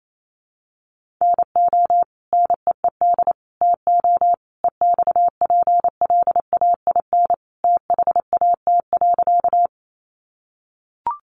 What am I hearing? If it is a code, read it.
Morse code, heard as a keyed tone, about 28 wpm, 700 Hz. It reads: NO NEED TO EXPLAIN THAT.
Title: Sentences from Top 1100 Words 28wpm